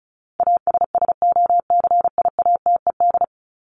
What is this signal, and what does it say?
Morse code, heard as a keyed tone, about 34 wpm, 700 Hz. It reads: ASSOCIATED
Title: ED Suffix - Top 100 Words with Base Words 35wpm